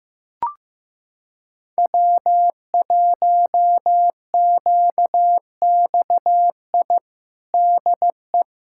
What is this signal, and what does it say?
Morse code, heard as a keyed tone, about 15 wpm, 700 Hz. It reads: W1QXI DE
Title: USA Callsign DE Callsign - 5 Characters - 15wpm